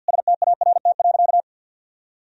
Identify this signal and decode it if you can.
Morse code, heard as a keyed tone, about 50 wpm, 700 Hz. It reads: START.
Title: Sentences from Top 300 Words and Abbreviations 50wpm